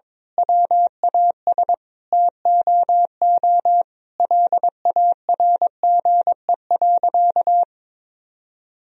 Morse code, heard as a keyed tone, about 22 wpm, 700 Hz. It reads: WAS TOO LARGE.
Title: Sentences from Top 1000 Words - Review Part 1 - 22wpm